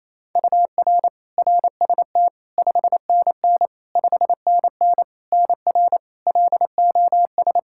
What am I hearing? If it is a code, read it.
Morse code, heard as a keyed tone, about 28 wpm, 700 Hz. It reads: UR RST 5NN 5NN NR LOS
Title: K1USN SST QSO Practice 28wpm